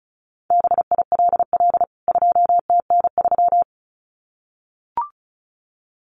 Morse code, heard as a keyed tone, about 34 wpm, 700 Hz. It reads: BILL 2TN3
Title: CWT Exchange 35wpm